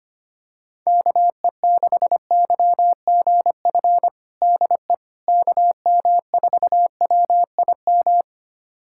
Morse code, heard as a keyed tone, about 25 wpm, 700 Hz. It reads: KE6YGF DE KM4WIM
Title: USA Callsign DE Callsign - 6 Characters - 25wpm